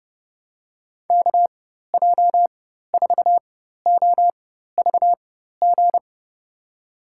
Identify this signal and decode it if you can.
Morse code, heard as a keyed tone, about 30 wpm, spaced as for 14 wpm, 700 Hz. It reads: KJ4OVG
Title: USA Call Signs - 6 Characters - 4x Character Spacing - 30wpm